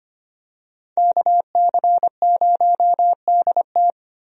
Morse code, heard as a keyed tone, about 25 wpm, 700 Hz. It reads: KC0DT
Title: USA Call Signs - 5 Characters - Repeated 3x - 25wpm